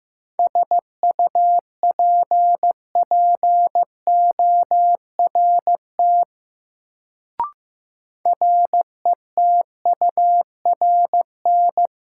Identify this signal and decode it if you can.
Morse code, heard as a keyed tone, about 15 wpm, 700 Hz. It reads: SUPPORT RETURN
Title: Sentences from Top 800 Words - New Word Drill - 15wpm